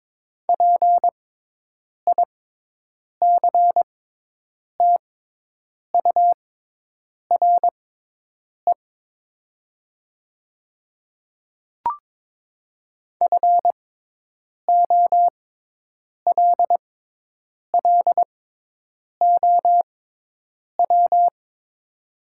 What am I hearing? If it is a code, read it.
Morse code, heard as a keyed tone, about 22 wpm, spaced as for 8 wpm, 700 Hz. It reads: PICTURE FOLLOW
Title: 6x Character Spacing - Top 200 Words - 22wpm